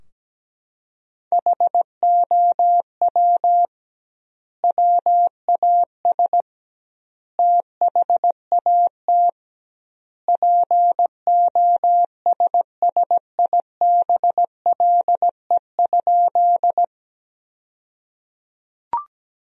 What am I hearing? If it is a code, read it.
Morse code, heard as a keyed tone, about 17 wpm, 700 Hz. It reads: HOW WAS THAT POSSIBLE?
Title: 2x Word-Spacing - Sentences from Top 500 Words - 17wpm